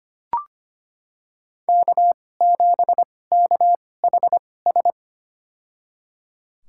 Morse code, heard as a keyed tone, about 25 wpm, spaced as for 18 wpm, 700 Hz. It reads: K7KHS
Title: USA Call Signs - 5 Characters - 2x Character Spacing - 25wpm